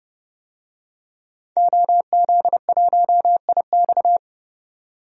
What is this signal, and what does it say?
Morse code, heard as a keyed tone, about 30 wpm, 700 Hz. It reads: OZ1IX
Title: DXCC-100 Call Signs - 5 Characters - 30wpm